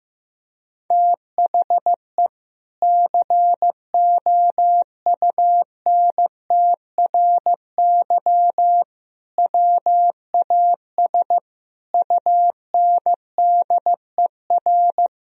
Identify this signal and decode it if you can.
Morse code, heard as a keyed tone, about 15 wpm, 700 Hz. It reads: THE COUNTRY WAS UNDER
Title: Sentences from Top 1000 Words - Review Part 1 - 15wpm